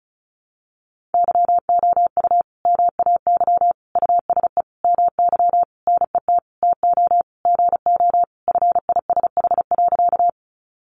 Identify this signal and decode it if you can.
Morse code, heard as a keyed tone, about 34 wpm, 700 Hz. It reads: YOU MAY USE MY NET TO GO FISH.
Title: Sentences from Top 1400 Words 35wpm